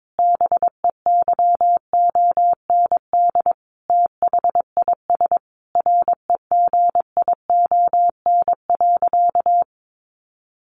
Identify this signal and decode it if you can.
Morse code, heard as a keyed tone, about 22 wpm, 700 Hz. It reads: BEYOND THIS REGION.
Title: Sentences from Top 1500 Words 22wpm